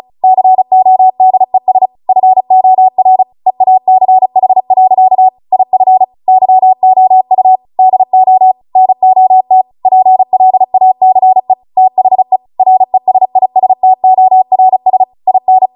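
Morse code, heard as a keyed tone, about 36 wpm, 750 Hz. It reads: CODES FOR EACH. IF YOU DO NOT PLACE THE RESISTORS IN
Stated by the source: Amateur radio Morse code practice archive transmission at 35 wpm.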